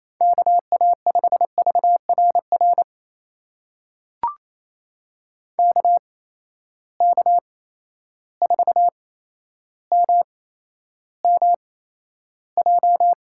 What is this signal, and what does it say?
Morse code, heard as a keyed tone, about 28 wpm, spaced as for 12 wpm, 700 Hz. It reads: KA5VRR KK4MMJ
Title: USA Call Signs - 6 Characters - 8x Character Spacing - 28wpm